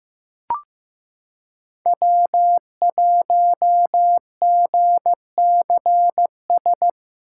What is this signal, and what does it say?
Morse code, heard as a keyed tone, about 15 wpm, 700 Hz. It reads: W1GCS
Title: K1USN SST QSO Practice - 15wpm